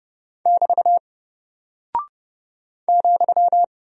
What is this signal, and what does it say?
Morse code, heard as a keyed tone, about 30 wpm, 700 Hz. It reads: <BT> ,
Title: Single Character (Letter:Number:Punctuation:Prosign) 30wpm